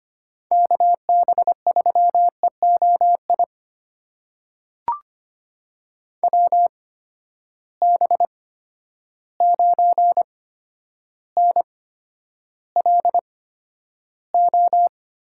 Morse code, heard as a keyed tone, about 25 wpm, spaced as for 11 wpm, 700 Hz. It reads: KB3EOI WB9NLO
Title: USA Call Signs - 6 Characters - 8x Character Spacing - 25wpm